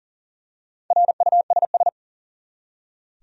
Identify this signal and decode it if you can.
Morse code, heard as a keyed tone, about 40 wpm, 700 Hz. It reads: RUSS
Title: USA Common Names - Max 4 Letters - 40wpm